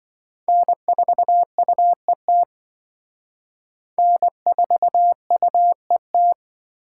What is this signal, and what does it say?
Morse code, heard as a keyed tone, about 25 wpm, 700 Hz. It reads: N4UET N4UET
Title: Speed Racing - USA 1x3 Call Signs 20wpm